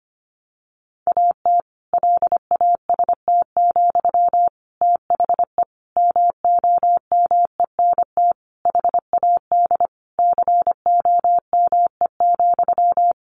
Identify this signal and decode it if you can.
Morse code, heard as a keyed tone, about 25 wpm, 700 Hz. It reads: AT LAST, THE MOMENT HAD COME,
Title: Sentences from Top 1500 Words 25wpm